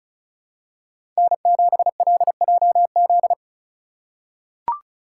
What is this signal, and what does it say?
Morse code, heard as a keyed tone, about 34 wpm, 700 Hz. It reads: N7LJZ
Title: USA Call Signs - 3x Character Spacing - 35wpm